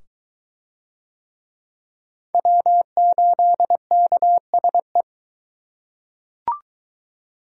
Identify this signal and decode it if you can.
Morse code, heard as a keyed tone, about 20 wpm, 700 Hz. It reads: W8KSE